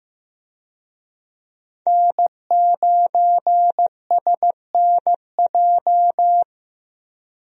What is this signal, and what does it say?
Morse code, heard as a keyed tone, about 15 wpm, 700 Hz. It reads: N9SNJ
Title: USA Call Signs - 5 Characters - 15wpm